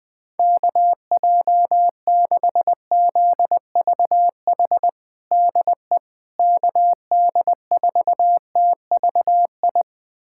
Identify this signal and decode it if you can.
Morse code, heard as a keyed tone, about 21 wpm, 700 Hz. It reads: KJ6ZVH DE KD4TVI
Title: USA Callsign DE Callsign 20wpm